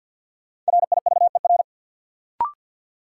Morse code, heard as a keyed tone, about 53 wpm, 700 Hz. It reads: RIVER
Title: Sentences from Top 300 Words - New Word Drill - 50wpm